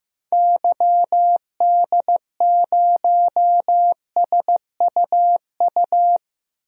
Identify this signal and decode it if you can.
Morse code, heard as a keyed tone, about 15 wpm, 700 Hz. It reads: YD0SUU
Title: DXCC-100 Call Signs - 6 Characters - Repeated 2x - 15wpm